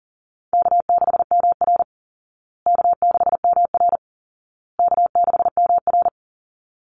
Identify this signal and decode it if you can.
Morse code, heard as a keyed tone, about 40 wpm, 700 Hz. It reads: K6MR K6MR K6MR
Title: Easy Call Signs Repeated 3x - 40wpm